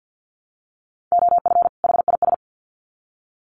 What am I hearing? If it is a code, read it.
Morse code, heard as a keyed tone, about 50 wpm, 700 Hz. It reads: OF HIS
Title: Sets of 2 Words 50wpm